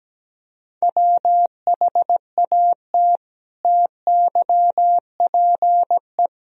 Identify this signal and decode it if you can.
Morse code, heard as a keyed tone, about 17 wpm, 700 Hz. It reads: WHAT TYPE
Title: Sentences from Top 1000 Words - Review Part 3 - 17wpm